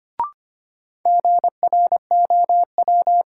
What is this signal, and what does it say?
Morse code, heard as a keyed tone, about 25 wpm, 700 Hz. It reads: GROW
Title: Sentences from Top 300 Words - New Word Drill - 25wpm